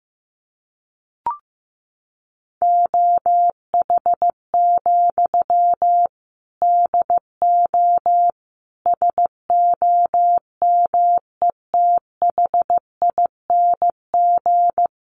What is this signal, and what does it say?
Morse code, heard as a keyed tone, about 15 wpm, 700 Hz. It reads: OH, DO SOMETHING
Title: Sentences from Top 600 Words 15wpm